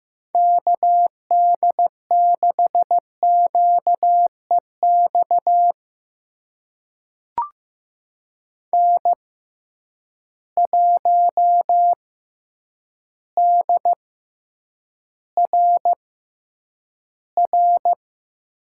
Morse code, heard as a keyed tone, about 15 wpm, spaced as for 8 wpm, 700 Hz. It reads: KD6QEX N1DRR
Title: USA Call Signs - 6x Character Spacing - 15wpm